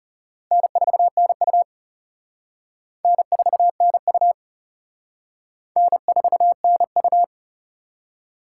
Morse code, heard as a keyed tone, about 34 wpm, 700 Hz. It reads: N4NU N4NU N4NU
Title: Speed Racing - USA 1x2 Call Signs 30wpm